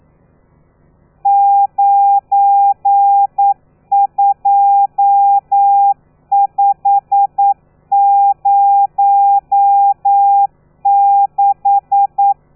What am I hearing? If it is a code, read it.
Morse code, heard as a keyed tone, about 9 wpm, 800 Hz. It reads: 92506
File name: telegrafia / i2tfj / Lez12 40